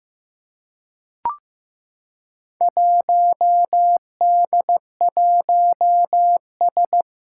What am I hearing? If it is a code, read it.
Morse code, heard as a keyed tone, about 15 wpm, 700 Hz. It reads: 1D1S
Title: Four Challenging Letter-Number Combinations 15wpm